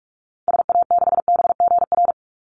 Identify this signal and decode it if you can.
Morse code, heard as a keyed tone, about 44 wpm, 700 Hz. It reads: SA6BGR
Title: Call Signs - 45wpm